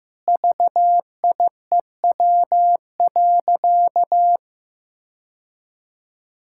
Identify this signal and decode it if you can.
Morse code, heard as a keyed tone, about 15 wpm, 700 Hz. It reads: VIEW.
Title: Sentences from Top 1000 Words - Review Part 6 - 15wpm